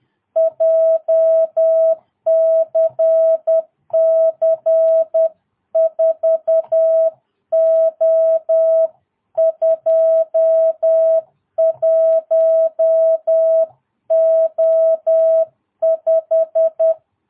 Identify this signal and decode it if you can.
Morse code, heard as a keyed tone, about 10 wpm, 650 Hz. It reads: JCC 4O21O5
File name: jh8cbh / 2022cw